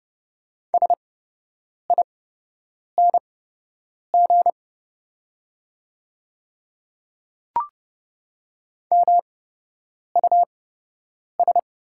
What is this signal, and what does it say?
Morse code, heard as a keyed tone, about 30 wpm, spaced as for 8 wpm, 690 Hz. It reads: SING MUS
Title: 8x Character Spacing - Top 500 Words - 30wpm